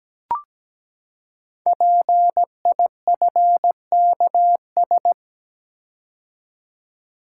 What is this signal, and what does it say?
Morse code, heard as a keyed tone, about 17 wpm, 700 Hz. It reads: PIFKS
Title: Code Groups - 5 Letters - 17wpm